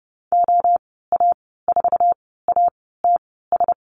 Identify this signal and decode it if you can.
Morse code, heard as a keyed tone, about 30 wpm, spaced as for 17 wpm, 700 Hz. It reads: OA4ATS
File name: Lesson 14g - Number 2 - Call Signs 30wpm